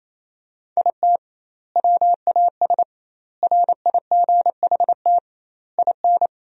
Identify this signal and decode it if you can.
Morse code, heard as a keyed tone, about 28 wpm, 700 Hz. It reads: IT WAS RIGHT IN
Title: 2x Word-Spacing - Sentences from Top 500 Words - 28wpm